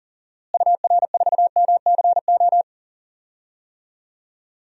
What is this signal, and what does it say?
Morse code, heard as a keyed tone, about 40 wpm, 700 Hz. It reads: UR4MCO
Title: DXCC-100 Call Signs - 6 Characters - Repeated 2x - 40wpm